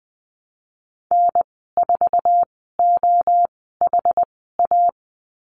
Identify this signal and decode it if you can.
Morse code, heard as a keyed tone, about 20 wpm, spaced as for 14 wpm, 700 Hz. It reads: N4OHA
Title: 082 Lesson 14g - Number 2 - Call Signs 20wpm